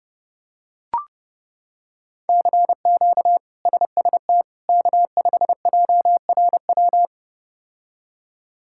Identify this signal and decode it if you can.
Morse code, heard as a keyed tone, about 30 wpm, 700 Hz. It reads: CQ SST K5JRW
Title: K1USN SST QSO Practice - 30wpm